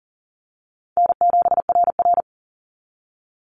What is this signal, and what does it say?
Morse code, heard as a keyed tone, about 40 wpm, 700 Hz. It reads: N7RR
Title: NA SOTA Activator Callsigns 40wpm